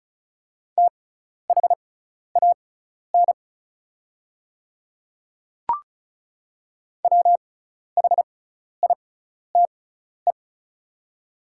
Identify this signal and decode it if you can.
Morse code, heard as a keyed tone, about 36 wpm, spaced as for 12 wpm, 690 Hz. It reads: THAN WHITE
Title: 6x Character Spacing - Top 300 Words - 35wpm